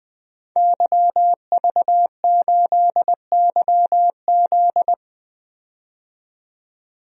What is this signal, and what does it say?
Morse code, heard as a keyed tone, about 20 wpm, 700 Hz. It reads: YV8YZ
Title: DXCC-100 Call Signs - 5 Characters - 20wpm